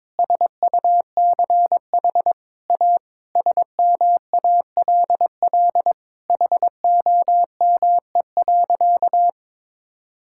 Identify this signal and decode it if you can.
Morse code, heard as a keyed tone, about 22 wpm, 700 Hz. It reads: SUCH A SMALL HOME.
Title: Sentences from Top 1000 Words - Review Part 6 - 22wpm